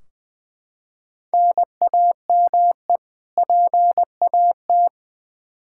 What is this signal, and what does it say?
Morse code, heard as a keyed tone, about 20 wpm, 700 Hz. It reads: NAME PAT
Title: Practice QSO - Exchange 1 - Beginning - 20wpm